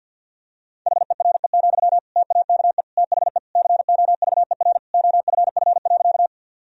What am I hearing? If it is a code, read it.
Morse code, heard as a keyed tone, about 53 wpm, 700 Hz. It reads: HERE, TAKE THE COVER OFF.
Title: Sentences from Top 1000 Words - Review Part 1 - 50wpm